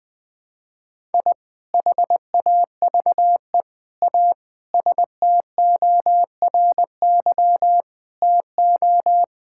Morse code, heard as a keyed tone, about 20 wpm, 700 Hz. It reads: I HAVE A STORY TO
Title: Sentences with 2-Word Phrases - 20wpm